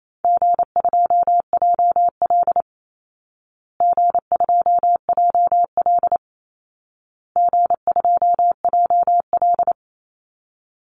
Morse code, heard as a keyed tone, about 28 wpm, 700 Hz. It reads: G2JL G2JL G2JL
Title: Easy Call Signs Repeated 3x - 28wpm